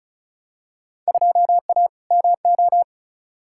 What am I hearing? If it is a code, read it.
Morse code, heard as a keyed tone, about 34 wpm, 700 Hz. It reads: 2A MO
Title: Field Day Exchanges Encore - 35wpm